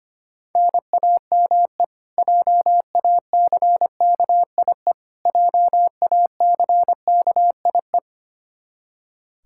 Morse code, heard as a keyed tone, about 25 wpm, 700 Hz. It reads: NAME JACKIE JACKIE
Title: Practice QSO - Exchange 1 - Beginning - 25wpm